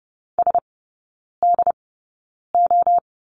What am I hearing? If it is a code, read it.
Morse code, heard as a keyed tone, about 30 wpm, spaced as for 9 wpm, 700 Hz. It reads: SDO
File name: course / 30:10wpm / 087 Lesson 15e - Number 5 - Three Characters 30wpm